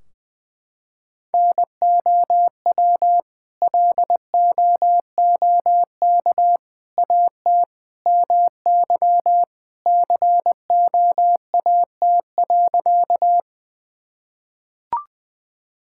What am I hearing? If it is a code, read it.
Morse code, heard as a keyed tone, about 20 wpm, 700 Hz. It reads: NOW LOOK AT MY COAT.
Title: Sentences from Top 1600 Words 20wpm